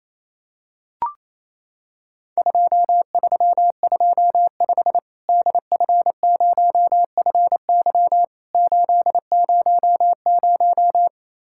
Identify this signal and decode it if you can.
Morse code, heard as a keyed tone, about 28 wpm, 700 Hz. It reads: 2325 DF0FY 800
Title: Worked All Europe DX Contest - QTC Exchange - 28wpm